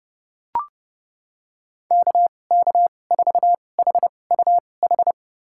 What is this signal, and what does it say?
Morse code, heard as a keyed tone, about 31 wpm, spaced as for 22 wpm, 700 Hz. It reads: KK4HUH
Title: USA Call Signs - 2x Character Spacing - 30wpm